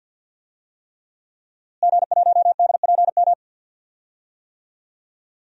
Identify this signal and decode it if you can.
Morse code, heard as a keyed tone, about 50 wpm, 700 Hz. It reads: G1DPM
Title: DXCC-100 Call Signs - 5 Characters - Repeated 3x - 50wpm